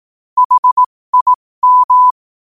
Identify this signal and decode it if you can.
Morse code, heard as a keyed tone, about 18 wpm, spaced as for 15 wpm, 1000 Hz. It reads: HIM
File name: qotdfiles / 15wpm